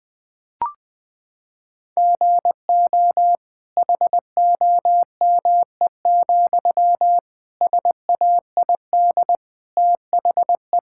Morse code, heard as a keyed tone, about 20 wpm, 700 Hz. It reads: GO HOME, SAID THE
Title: Sentences from Top 1600 Words 20wpm